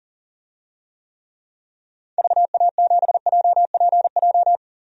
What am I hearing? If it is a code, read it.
Morse code, heard as a keyed tone, about 40 wpm, 700 Hz. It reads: VA7JPJ
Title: DXCC-100 Call Signs - 6 Characters - 40wpm